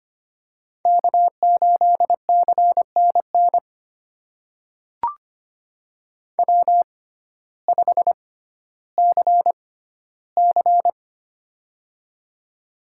Morse code, heard as a keyed tone, about 25 wpm, spaced as for 14 wpm, 700 Hz. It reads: K8CNN W5CC
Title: USA Call Signs - 6x Character Spacing - 25wpm